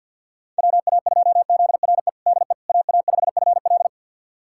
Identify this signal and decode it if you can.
Morse code, heard as a keyed tone, about 46 wpm, 700 Hz. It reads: WA2ZRE DE AA5FL
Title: USA Callsign DE Callsign 50wpm